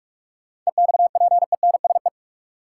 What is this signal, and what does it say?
Morse code, heard as a keyed tone, about 44 wpm, 700 Hz. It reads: EXPENSE
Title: Sets of 4 Words - New Word Drill - 45wpm